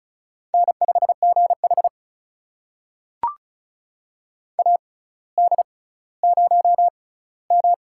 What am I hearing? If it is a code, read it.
Morse code, heard as a keyed tone, about 36 wpm, spaced as for 18 wpm, 700 Hz. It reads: N5GH AD0M
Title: USA Call Signs - 4 Characters - 6x Character Spacing - 35wpm